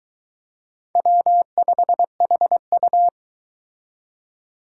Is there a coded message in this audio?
Yes: Morse code from 1.0 to 3.1 s, heard as a keyed tone, about 22 wpm, 700 Hz. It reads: W5HU